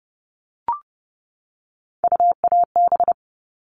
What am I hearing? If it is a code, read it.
Morse code, heard as a keyed tone, about 30 wpm, 700 Hz. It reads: UAB
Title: All Cut Numbers - 3-Digit 30wpm